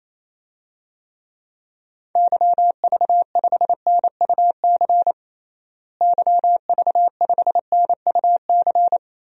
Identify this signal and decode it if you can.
Morse code, heard as a keyed tone, about 28 wpm, 700 Hz. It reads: YV5NUC YV5NUC
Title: DXCC-100 Call Signs - 6 Characters - Repeated 2x - 28wpm